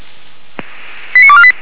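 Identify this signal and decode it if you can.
Five-tone selective call (ZVEI), heard as quick runs of five tones, tones of about 70 ms. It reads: 89248
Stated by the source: ZVEI five-tone selective call received by SDR, from the Feuerwehr Orlamünde fire brigade server.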